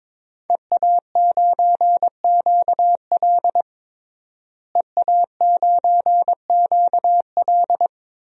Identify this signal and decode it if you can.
Morse code, heard as a keyed tone, about 22 wpm, 700 Hz. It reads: EA9QL EA9QL
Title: DXCC-100 Call Signs - 5 Characters - Repeated 3x - 22wpm